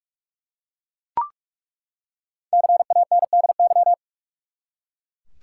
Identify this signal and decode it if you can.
Morse code, heard as a keyed tone, about 47 wpm, 700 Hz. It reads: CANDY
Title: Five Letter Words - 45wpm